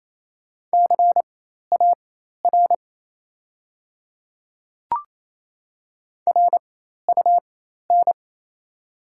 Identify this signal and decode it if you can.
Morse code, heard as a keyed tone, about 28 wpm, spaced as for 13 wpm, 700 Hz. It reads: CAR RUN
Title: 4x Character Spacing - Top 300 Words - 28wpm